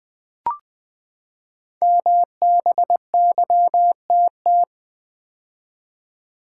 Morse code, heard as a keyed tone, about 20 wpm, 700 Hz. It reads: MBYTT
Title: Code Groups - 5 Letter:Numbers - 20wpm